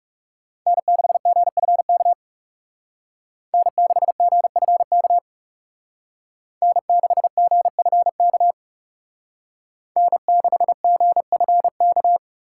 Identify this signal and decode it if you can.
Morse code, heard as a keyed tone, about 36 wpm, 700 Hz. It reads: N6GFK N6GFK N6GFK N6GFK
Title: Speed Racing - USA 1x3 Call Signs 30wpm